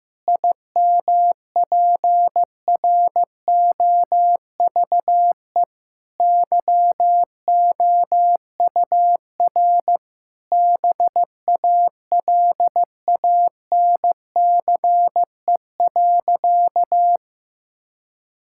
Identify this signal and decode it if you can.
Morse code, heard as a keyed tone, about 15 wpm, 700 Hz. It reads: IMPROVE YOUR BALANCE.
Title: Sentences from Top 1000 Words - Review Part 6 - 15wpm